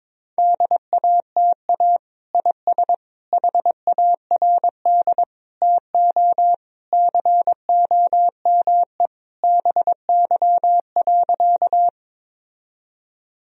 Morse code, heard as a keyed tone, about 22 wpm, 700 Hz. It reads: DATA IS HARD TO COME BY.
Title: Sentences from Top 1000 Words - Review Part 3 - 22wpm